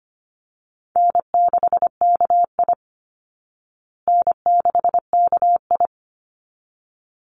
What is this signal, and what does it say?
Morse code, heard as a keyed tone, about 25 wpm, 700 Hz. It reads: N6KI N6KI
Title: Call Signs Repeated 3x - 25wpm